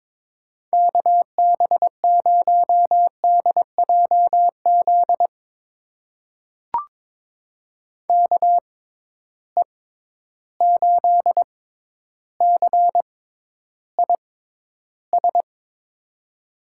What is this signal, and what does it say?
Morse code, heard as a keyed tone, about 22 wpm, spaced as for 11 wpm, 700 Hz. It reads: KB0DJZ KE8CIS
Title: USA Call Signs - 6 Characters - 6x Character Spacing - 22wpm